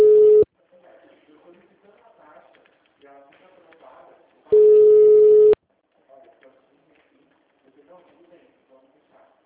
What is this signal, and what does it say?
Ringback tone (Europe, 425 Hz), with speech.
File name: out- -2013- - - .38694